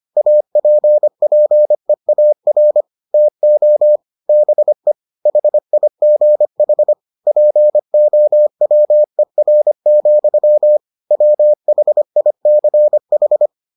Morse code, heard as a keyed tone, about 25 wpm, 600 Hz. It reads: APPEAR TO BE HIGH POWER, WHICH